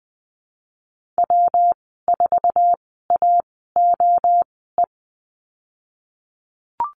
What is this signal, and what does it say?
Morse code, heard as a keyed tone, about 20 wpm, spaced as for 15 wpm, 700 Hz. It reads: W4AOE
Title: 167 Lesson 27g - Number 7 - Call Signs 20wpm